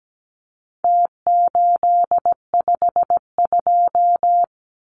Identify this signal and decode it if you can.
Morse code, heard as a keyed tone, about 17 wpm, 700 Hz. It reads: T852
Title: Numbers - Contest Style - 4-Digit 17wpm